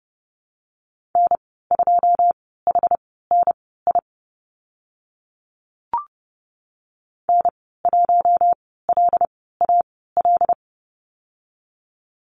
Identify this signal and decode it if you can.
Morse code, heard as a keyed tone, about 31 wpm, spaced as for 17 wpm, 690 Hz. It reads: N2HNI N1LAL
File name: Lesson 15g - Number 5 - Call Signs 30wpm